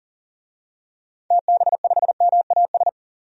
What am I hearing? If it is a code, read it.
Morse code, heard as a keyed tone, about 40 wpm, 700 Hz. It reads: TB5MAS
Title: DXCC-100 Call Signs - 6 Characters - 40wpm